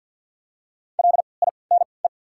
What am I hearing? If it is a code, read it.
Morse code, heard as a keyed tone, about 50 wpm, spaced as for 28 wpm, 700 Hz. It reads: LINE
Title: 3x Character Spacing - Top 200 Words - 50wpm